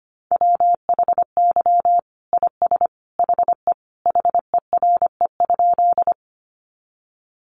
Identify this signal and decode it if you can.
Morse code, heard as a keyed tone, about 25 wpm, 700 Hz. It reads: WHY IS HE HERE?
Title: Sentences from Top 200 Words 25wpm